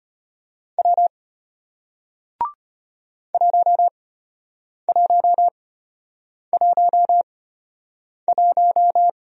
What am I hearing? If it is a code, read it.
Morse code, heard as a keyed tone, about 33 wpm, spaced as for 8 wpm, 700 Hz. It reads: W 1111